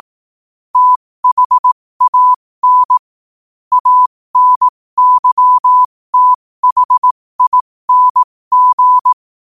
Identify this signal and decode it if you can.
Morse code, heard as a keyed tone, about 18 wpm, spaced as for 15 wpm, 1000 Hz. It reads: THAN ANYTHING
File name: qotdfiles / 15wpm